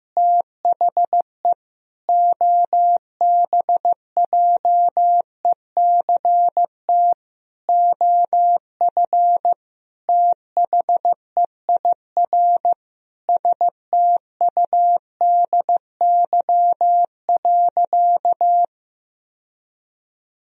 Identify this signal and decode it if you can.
Morse code, heard as a keyed tone, about 15 wpm, 700 Hz. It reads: THE OBJECT OF THEIR STUDY.
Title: Sentences from Top 1000 Words - Review Part 3 - 15wpm